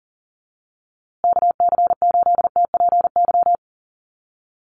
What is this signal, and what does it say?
Morse code, heard as a keyed tone, about 40 wpm, 700 Hz. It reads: KC8TPY
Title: USA Call Signs Repeated 3x 40wpm